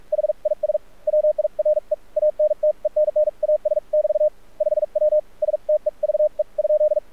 Morse code, heard as a keyed tone, about 44 wpm, spaced as for 38 wpm, 610 Hz. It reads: HIS WIRE ANTENNAS <BT> 5W STEVE ?
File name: archived / CRSnet